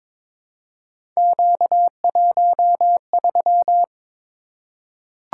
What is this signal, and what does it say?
Morse code, heard as a keyed tone, about 22 wpm, 700 Hz. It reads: Q13
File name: practice / 22wpm / Three Letter-Number Combinations - 22wpm